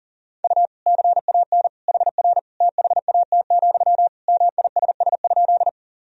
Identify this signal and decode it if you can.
Morse code, heard as a keyed tone, about 37 wpm, 700 Hz. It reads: U CAN HR THAT, MISS?
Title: Sentences from Top 500 Words and Abbreviations 40wpm